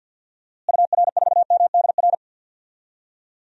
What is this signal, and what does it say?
Morse code, heard as a keyed tone, about 51 wpm, 700 Hz. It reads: UR4MDR
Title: DXCC-100 Call Signs - 6 Characters - Repeated 3x - 50wpm